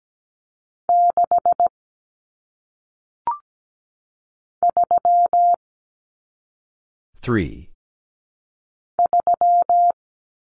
Morse code, heard as a keyed tone, about 17 wpm, 700 Hz. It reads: 6 3 3
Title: Common Cut Numbers - 1-Digit 17wpm